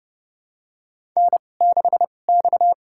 Morse code, heard as a keyed tone, about 30 wpm, spaced as for 22 wpm, 700 Hz. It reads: N6X
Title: USA Call Signs - 5 Characters - 2x Character Spacing - 30wpm